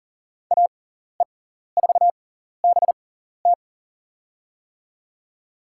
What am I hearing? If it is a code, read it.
Morse code, heard as a keyed tone, about 40 wpm, spaced as for 14 wpm, 700 Hz. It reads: AE4BT
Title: USA Call Signs - 5 Characters - 6x Character Spacing - 40wpm